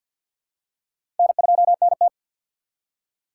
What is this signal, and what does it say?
Morse code, heard as a keyed tone, about 46 wpm, 700 Hz. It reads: N2NT
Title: USA Call Signs - 4x Character Spacing - 50wpm